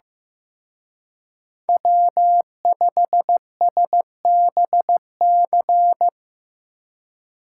Morse code, heard as a keyed tone, about 18 wpm, 700 Hz. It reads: W5SBC
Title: Speed Racing - USA 1x3 Call Signs 15wpm